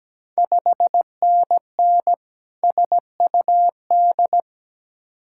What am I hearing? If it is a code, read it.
Morse code, heard as a keyed tone, about 17 wpm, 700 Hz. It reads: 5NN SUD
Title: Ontario QSO Party Exchange 17wpm